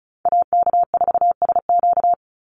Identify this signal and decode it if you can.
Morse code, heard as a keyed tone, about 34 wpm, 700 Hz. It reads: AK4SQ
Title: Call Signs Repeated 3x - 35wpm